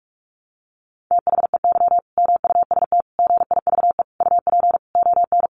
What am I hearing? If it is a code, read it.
Morse code, heard as a keyed tone, about 44 wpm, 700 Hz. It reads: THEY MUST GIVE UP ON
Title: Sentences from Top 1100 Words 45wpm